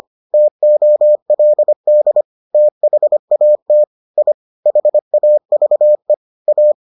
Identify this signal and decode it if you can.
Morse code, heard as a keyed tone, about 25 wpm, 600 Hz. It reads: TOLD THAT I HAVE A